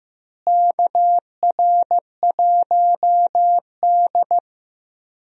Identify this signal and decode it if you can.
Morse code, heard as a keyed tone, about 15 wpm, 700 Hz. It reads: KR1D
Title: USA Call Signs - 4 Characters - Repeated 3x - 15wpm